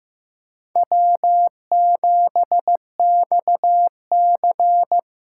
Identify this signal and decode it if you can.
Morse code, heard as a keyed tone, about 18 wpm, 700 Hz. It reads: W7XC
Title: Speed Racing - USA 1x2 Call Signs 15wpm